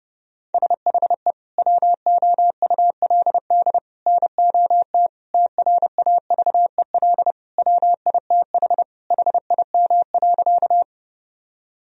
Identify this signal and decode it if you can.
Morse code, heard as a keyed tone, about 32 wpm, 700 Hz. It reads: SHE WOULD NOT TRAVEL WITH HIM.
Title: Sentences from Top 1000 Words - Review Part 2 - 30wpm